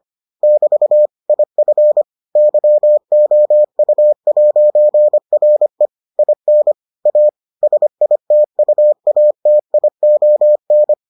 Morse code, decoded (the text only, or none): <BT> IF YOU'RE IN A SITUATION